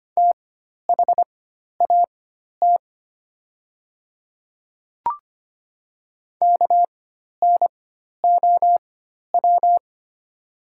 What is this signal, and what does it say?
Morse code, heard as a keyed tone, about 25 wpm, spaced as for 12 wpm, 700 Hz. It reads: THAT KNOW
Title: 4x Character Spacing - Top 100 Words - 25wpm